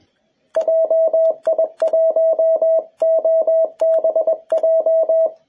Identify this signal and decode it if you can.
Morse code, heard as a keyed tone, about 21 wpm, 650 Hz, with radio noise. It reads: JI1OBJ
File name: jh8cbh / 2021cw